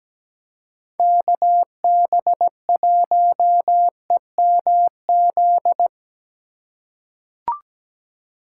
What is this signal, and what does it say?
Morse code, heard as a keyed tone, about 17 wpm, 700 Hz. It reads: KB1EMZ
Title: USA Call Signs - 6 Characters - 2x Character Spacing - 17wpm